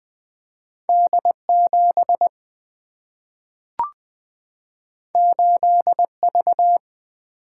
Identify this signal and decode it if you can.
Morse code, heard as a keyed tone, about 20 wpm, 700 Hz. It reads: D 7 8 V
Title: Single-Letter Single-Number 20wpm